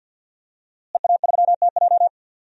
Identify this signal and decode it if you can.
Morse code, heard as a keyed tone, about 51 wpm, 700 Hz. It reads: EA3TJ
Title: DXCC-100 Call Signs - 5 Characters - Repeated 3x - 50wpm